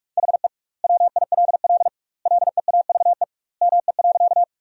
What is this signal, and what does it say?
Morse code, heard as a keyed tone, about 47 wpm, 700 Hz. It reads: HE WILL LEAVE ME.
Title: 2x Word-Spacing - Sentences from Top 400 Words - 45wpm